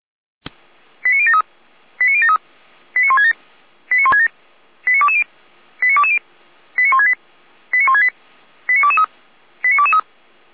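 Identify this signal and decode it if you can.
Five-tone selective call (ZVEI), heard as quick runs of five tones, tones of about 70 ms. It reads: 89073 89073 89167 89167 89220 89220 89168 89168 89223 89223
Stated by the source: ZVEI five-tone selective call received on an SDR.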